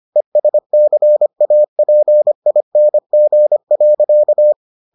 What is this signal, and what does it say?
Morse code, heard as a keyed tone, about 25 wpm, 600 Hz. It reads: ESCAPING.